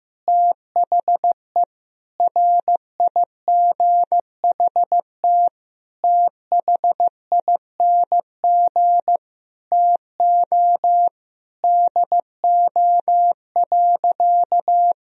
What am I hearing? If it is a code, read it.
Morse code, heard as a keyed tone, about 15 wpm, 700 Hz. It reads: THE RIGHT THING TO DO.
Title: Sentences from Top 1000 Words - Review Part 4 - 15wpm